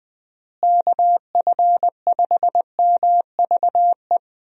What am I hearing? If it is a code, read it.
Morse code, heard as a keyed tone, about 20 wpm, 700 Hz. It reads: KF5MVE
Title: USA Call Signs - 6 Characters - Repeated 3x - 20wpm